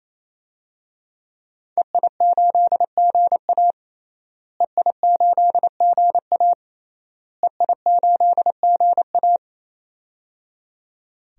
Morse code, heard as a keyed tone, about 28 wpm, 700 Hz. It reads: EI8GA EI8GA EI8GA
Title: DXCC-100 Call Signs - 5 Characters - Repeated 3x - 28wpm